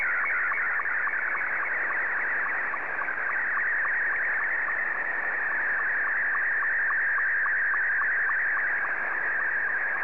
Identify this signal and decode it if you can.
SSTV (slow-scan television), Scottie 2.